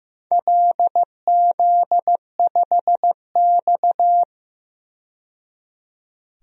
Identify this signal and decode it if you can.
Morse code, heard as a keyed tone, about 15 wpm, 700 Hz. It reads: LZ5X
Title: DXCC-100 Call Signs - 4 Characters - 15wpm